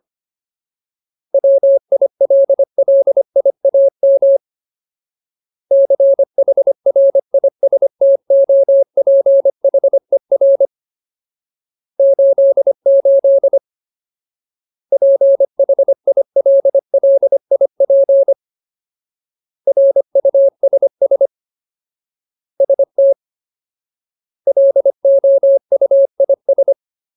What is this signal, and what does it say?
Morse code, heard as a keyed tone, about 25 wpm, 550 Hz. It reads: WILLIAM CHRISTOPHER 88 PHILLIP RUSS ST LOUIS
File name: Ultimate Word List / 25WPM Ultimate